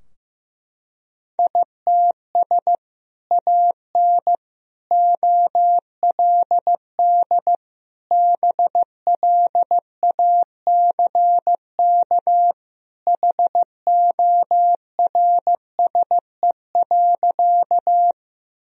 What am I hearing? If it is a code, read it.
Morse code, heard as a keyed tone, about 16 wpm, 700 Hz. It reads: ITS AN OLD BLACK HORSE.
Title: Sentences from Top 1000 Words - Review Part 2 - 15wpm